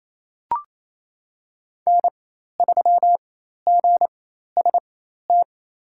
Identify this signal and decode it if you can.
Morse code, heard as a keyed tone, about 28 wpm, spaced as for 13 wpm, 700 Hz. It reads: N3GST
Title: USA Call Signs - 4x Character Spacing - 28wpm